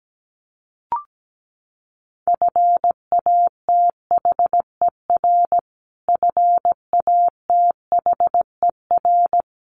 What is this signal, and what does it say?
Morse code, heard as a keyed tone, about 17 wpm, 700 Hz. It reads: FATHER FATHER
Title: Top 500 Words Repeated 17wpm